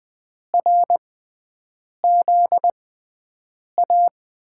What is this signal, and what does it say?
Morse code, heard as a keyed tone, about 20 wpm, spaced as for 7 wpm, 700 Hz. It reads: RZA
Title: USA Call Signs - 5 Characters - 6x Character Spacing - 20wpm